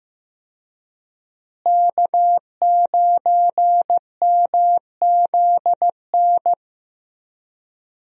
Morse code, heard as a keyed tone, about 15 wpm, 700 Hz. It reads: K9MZN
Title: USA Call Signs - 5 Characters - Repeated 3x - 15wpm